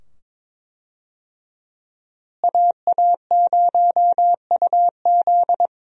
Morse code, heard as a keyed tone, about 22 wpm, 700 Hz. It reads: AA0UZ
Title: USA Call Signs - 5 Characters - Repeated 2x - 22wpm